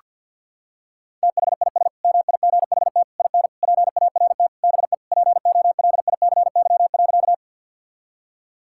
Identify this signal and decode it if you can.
Morse code, heard as a keyed tone, about 48 wpm, 700 Hz. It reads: THIS MIGHT IN PART BE POLICY.
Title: Sentences from Top 1000 Words - Review Part 5 - 50wpm